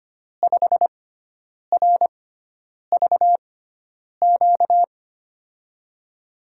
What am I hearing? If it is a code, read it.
Morse code, heard as a keyed tone, about 25 wpm, spaced as for 9 wpm, 700 Hz. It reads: 5RVQ
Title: USA Call Signs - 6 Characters - 6x Character Spacing - 25wpm